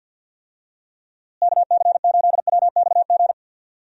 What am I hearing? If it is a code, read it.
Morse code, heard as a keyed tone, about 46 wpm, 700 Hz. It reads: KC8WXG
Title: USA Call Signs - 6 Characters - Repeated 2x - 50wpm